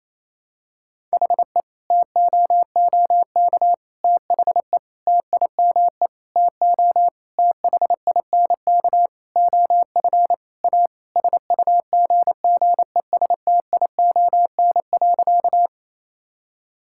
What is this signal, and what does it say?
Morse code, heard as a keyed tone, about 28 wpm, 700 Hz. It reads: HE TOOK THE TIME TO THINK OF A SUGGESTION.
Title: Sentences from Top 1600 Words 28wpm